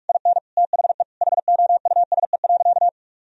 Morse code, heard as a keyed tone, about 46 wpm, 700 Hz. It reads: IN THE HOUSE.
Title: Sentences from Top 1000 Words - Review Part 4 - 45wpm